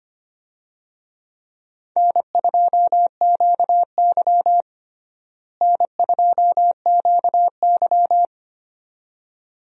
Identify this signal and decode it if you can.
Morse code, heard as a keyed tone, about 25 wpm, 700 Hz. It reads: N2QY N2QY
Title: USA Call Signs - 4 Characters - Repeated 2x - 25wpm